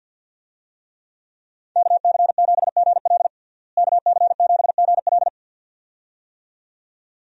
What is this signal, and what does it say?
Morse code, heard as a keyed tone, about 50 wpm, 700 Hz. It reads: KC7GL KC7GL
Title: USA Call Signs - 5 Characters - Repeated 2x - 50wpm